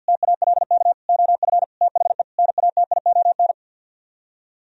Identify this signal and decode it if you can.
Morse code, heard as a keyed tone, about 64 wpm, 700 Hz. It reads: TALK OF THE NATION